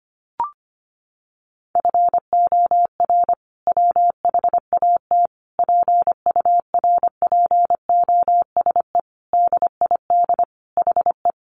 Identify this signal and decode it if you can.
Morse code, heard as a keyed tone, about 25 wpm, 700 Hz. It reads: FOR WHAT PURPOSE DID HE
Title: Sentences from Top 800 Words 25wpm